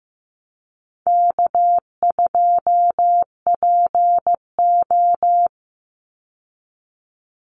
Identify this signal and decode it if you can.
Morse code, heard as a keyed tone, about 15 wpm, 700 Hz. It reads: K2PO
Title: Easy Call Signs Repeated 2x - 15wpm